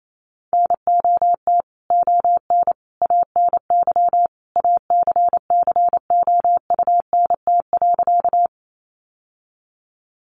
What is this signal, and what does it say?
Morse code, heard as a keyed tone, about 28 wpm, 700 Hz. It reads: NOT ON ANY ACCOUNT.
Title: Sentences from Top 900 Words 28wpm